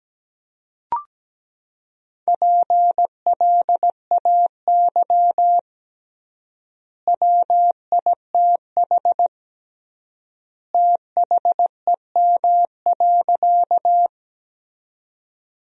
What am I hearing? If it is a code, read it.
Morse code, heard as a keyed tone, about 17 wpm, 700 Hz. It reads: PLAY WITH THEM.
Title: 3x Word-Spacing - Sentences from Top 200 Words - 17wpm